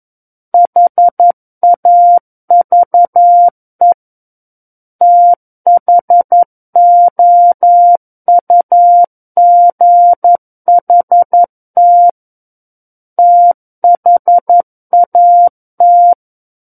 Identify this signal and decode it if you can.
Morse code, heard as a keyed tone, about 11 wpm, 700 Hz. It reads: HAVE THOUGHT THAT
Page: SKCC - War of the Worlds - 02-Chapter 2 - 11 WPM